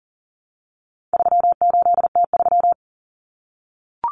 Morse code, heard as a keyed tone, about 40 wpm, 700 Hz. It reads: 38T3
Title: Common Cut Numbers - 4-Digit 40wpm